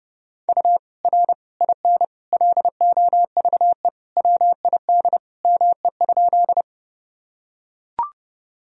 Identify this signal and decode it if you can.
Morse code, heard as a keyed tone, about 32 wpm, 700 Hz. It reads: U R IN LOVE WID ME?
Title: Sentences from Top 400 Words and Abbreviations 30wpm